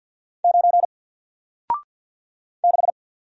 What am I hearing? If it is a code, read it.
Morse code, heard as a keyed tone, about 48 wpm, 700 Hz. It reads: 9 6
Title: Single Letter-Number Alt 50wpm